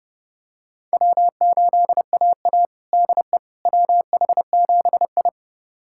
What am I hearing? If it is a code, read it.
Morse code, heard as a keyed tone, about 30 wpm, 700 Hz. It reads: W8AA DE WH7I
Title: USA Callsign DE Callsign - 4 Characters - 30wpm